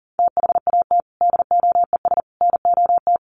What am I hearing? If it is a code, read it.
Morse code, heard as a keyed tone, about 40 wpm, 700 Hz. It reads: THAT DOES NOT
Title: Sets of 3 Words 40wpm